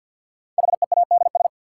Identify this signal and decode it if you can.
Morse code, heard as a keyed tone, about 53 wpm, 700 Hz. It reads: HEADS
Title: Sentences from Top 1000 Words - Review Part 3 - New Word Drill - 50wpm